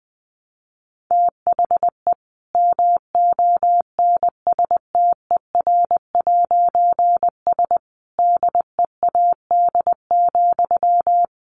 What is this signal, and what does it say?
Morse code, heard as a keyed tone, about 20 wpm, 700 Hz. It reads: THE MONSTER'S DEAD,
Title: A Sound Of Thunder 20wpm